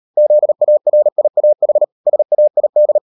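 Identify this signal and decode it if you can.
Morse code, heard as a keyed tone, about 38 wpm, 600 Hz. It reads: ZARIAH SAID
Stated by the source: Morse code practice at 38 WPM, sending Ken Follett's novel Never in English, from CW med Gnister.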